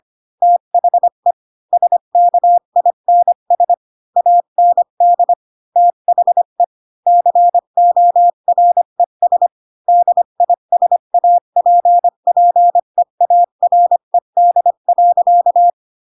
Morse code, heard as a keyed tone, about 25 wpm, 700 Hz. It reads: THE SKINS AND THE CORES DISAPPEARED.
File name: Pinocchio / Pinocchio ch7 20 25 ik1hge